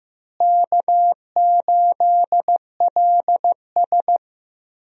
Morse code, heard as a keyed tone, about 15 wpm, 700 Hz. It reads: K8LS
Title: USA Call Signs - 4 Characters - Repeated 2x - 15wpm